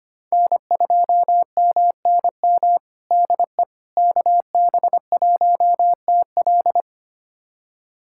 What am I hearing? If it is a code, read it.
Morse code, heard as a keyed tone, about 25 wpm, 700 Hz. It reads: N2MNM DE KB1TL
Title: USA Callsign DE Callsign - 5 Characters - 25wpm